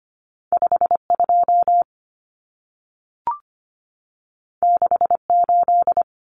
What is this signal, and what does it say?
Morse code, heard as a keyed tone, about 25 wpm, 700 Hz. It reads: 52 68
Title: Common Cut Numbers - 2-Digit 25wpm